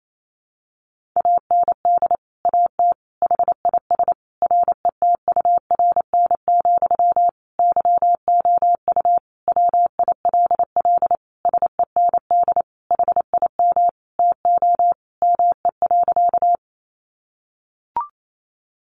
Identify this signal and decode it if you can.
Morse code, heard as a keyed tone, about 28 wpm, 700 Hz. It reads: AND AT HIS RETURN, YOU WILL SEND HIM TO ME.
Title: Sentences from Top 1500 Words 28wpm